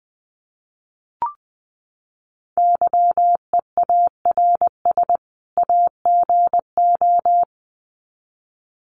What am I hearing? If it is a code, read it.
Morse code, heard as a keyed tone, about 20 wpm, 700 Hz. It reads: YEARS AGO
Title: Sets of 2 Words 20wpm